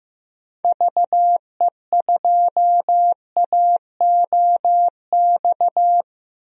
Morse code, heard as a keyed tone, about 15 wpm, 700 Hz. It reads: VE2AOX
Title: DXCC-100 Call Signs - 6 Characters - 15wpm